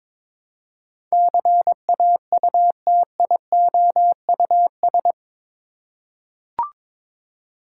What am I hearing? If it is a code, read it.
Morse code, heard as a keyed tone, about 22 wpm, 700 Hz. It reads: CAUTIOUS
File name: practice / 22wpm / Eight Letter Words - 22wpm